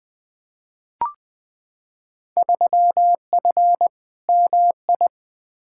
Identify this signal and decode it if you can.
Morse code, heard as a keyed tone, about 20 wpm, 700 Hz. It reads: 3F MI
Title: Field Day Exchanges - 20wpm